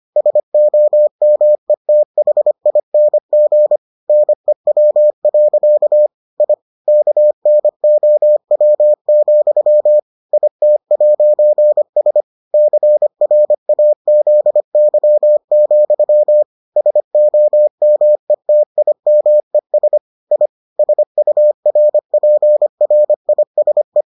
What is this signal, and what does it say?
Morse code, heard as a keyed tone, about 25 wpm, 600 Hz. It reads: SOMETHING NEW. I KNOW, IT'S CRAZY, SOMETIMES I SURPRISE